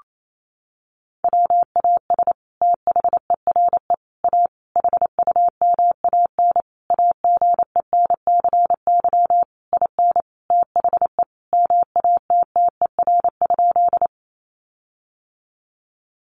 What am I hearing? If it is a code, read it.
Morse code, heard as a keyed tone, about 28 wpm, 700 Hz. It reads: WAS THERE A HUMAN AGENCY IN THE MATTER?
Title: Sentences from Top 1300 Words 28wpm